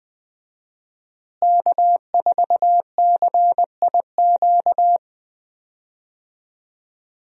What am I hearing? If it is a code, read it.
Morse code, heard as a keyed tone, about 20 wpm, 700 Hz. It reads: K4CIQ
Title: USA Call Signs - 5 Characters - Repeated 3x - 20wpm